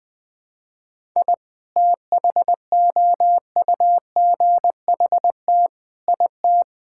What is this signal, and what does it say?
Morse code, heard as a keyed tone, about 21 wpm, 700 Hz. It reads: I THOUGHT IT
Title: Build A Phrase - 5-Word Phrases - 20wpm